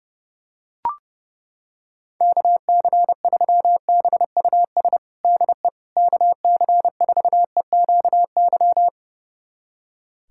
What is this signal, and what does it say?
Morse code, heard as a keyed tone, about 30 wpm, 700 Hz. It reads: KC3BUS DE KC4EQY
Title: USA Callsign DE Callsign 30wpm